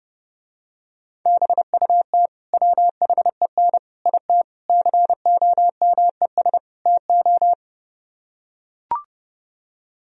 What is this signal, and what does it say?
Morse code, heard as a keyed tone, about 30 wpm, 700 Hz. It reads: BUT WHEN IT COMES TO